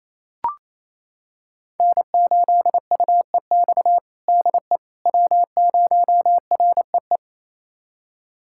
Morse code, heard as a keyed tone, about 28 wpm, 700 Hz. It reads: N8UEX DE W0REE
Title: USA Callsign DE Callsign - 5 Characters - 28wpm